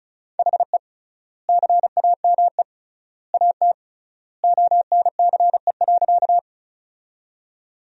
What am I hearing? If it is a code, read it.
Morse code, heard as a keyed tone, about 36 wpm, 700 Hz. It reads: HE CAME AT ONCE.
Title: 3x Word-Spacing - Sentences from Top 300 Words - 35wpm